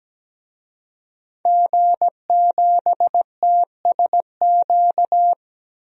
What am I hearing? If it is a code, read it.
Morse code, heard as a keyed tone, about 17 wpm, 700 Hz. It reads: G7TSQ
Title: DXCC-100 Call Signs - 5 Characters - 17wpm